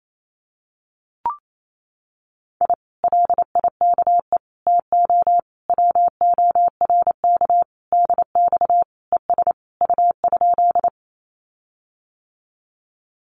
Morse code, heard as a keyed tone, about 28 wpm, 700 Hz. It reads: I LIKE TO WORK DX ES U?
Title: Common QSO Phrases Encore 28wpm